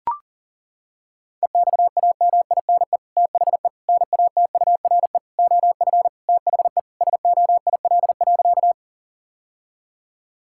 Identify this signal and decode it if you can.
Morse code, heard as a keyed tone, about 40 wpm, 700 Hz. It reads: EXAMINE THE NATURE OF THE SOIL.
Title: Sentences from Top 1600 Words 40wpm